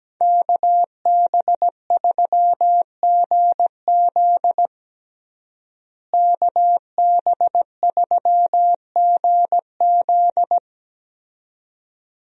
Morse code, heard as a keyed tone, about 17 wpm, 700 Hz. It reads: KB3GZ KB3GZ
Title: USA Call Signs - 5 Characters - Repeated 2x - 17wpm